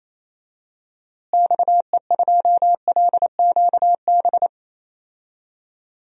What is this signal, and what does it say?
Morse code, heard as a keyed tone, about 28 wpm, 700 Hz. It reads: XE2LQB
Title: DXCC-100 Call Signs - 6 Characters - Repeated 3x - 28wpm